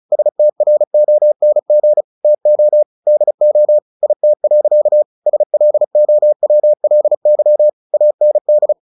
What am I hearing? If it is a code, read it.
Morse code, heard as a keyed tone, about 34 wpm, 600 Hz. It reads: STRONG TO DO IT. SLOWLY AND
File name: the old man and the sea / the old man and the sea 1 35wpm